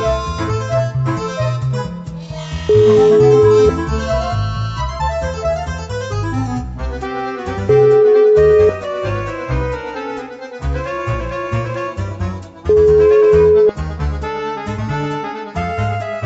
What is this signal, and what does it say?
Ringback tone (Europe, 425 Hz).